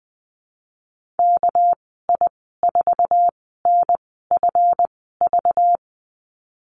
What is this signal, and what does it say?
Morse code, heard as a keyed tone, about 20 wpm, spaced as for 15 wpm, 700 Hz. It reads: KI4NFV
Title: 200 Lesson 32g - Letter k - Call Signs 20wpm